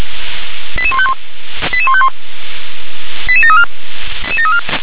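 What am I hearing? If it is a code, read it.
Five-tone selective call (ZVEI), heard as quick runs of five tones, tones of about 70 ms. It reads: 88151 88151 88734
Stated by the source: ZVEI five-tone selective call received by SDR.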